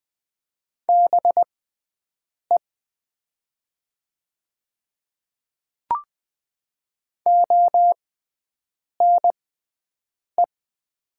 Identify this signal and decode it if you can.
Morse code, heard as a keyed tone, about 20 wpm, spaced as for 7 wpm, 700 Hz. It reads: BE ONE
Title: 6x Character Spacing - Top 100 Words - 20wpm